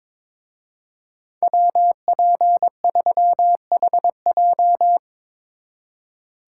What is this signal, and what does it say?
Morse code, heard as a keyed tone, about 25 wpm, 700 Hz. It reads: WP3HJ
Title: Speed Racing - USA 2x2 Call Signs 22wpm